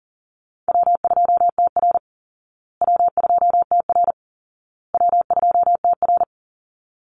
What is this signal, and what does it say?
Morse code, heard as a keyed tone, about 40 wpm, 700 Hz. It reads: W2TR W2TR W2TR
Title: Easy Call Signs Repeated 3x - 40wpm